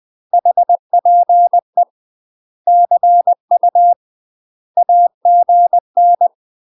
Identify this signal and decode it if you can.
Morse code, heard as a keyed tone, about 20 wpm, 700 Hz. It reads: HPE CU AGN
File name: UK MASTER QSOs TRAINING / 20WPM / UK MASTER QSOs 0047